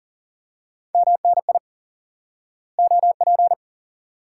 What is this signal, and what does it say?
Morse code, heard as a keyed tone, about 39 wpm, 700 Hz. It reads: MNI OP
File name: IQE Course 3 - Lesson 89 - YAGI - Two Elements - 40wpm